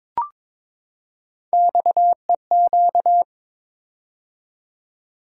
Morse code, heard as a keyed tone, about 22 wpm, 700 Hz. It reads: XEQ
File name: practice / 22wpm / Three Letter-Number Combinations - 22wpm